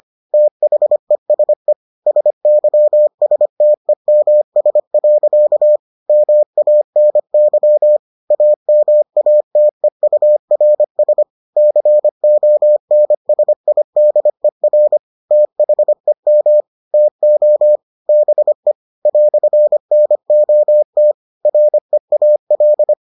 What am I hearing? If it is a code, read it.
Morse code, heard as a keyed tone, about 25 wpm, 600 Hz. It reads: THESE SYSTEMS. MANY AMATEURS CONSIDER THEM TO BE "NOT REAL